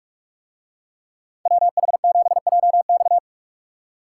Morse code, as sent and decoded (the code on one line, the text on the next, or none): .-- .... --... .--- -..-
WH7JX